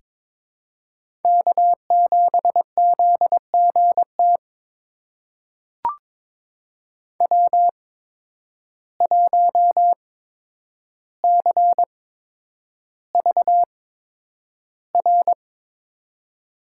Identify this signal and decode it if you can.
Morse code, heard as a keyed tone, about 22 wpm, spaced as for 9 wpm, 700 Hz. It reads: K7ZGT W1CVR
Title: USA Call Signs - 5 Characters - 8x Character Spacing - 22wpm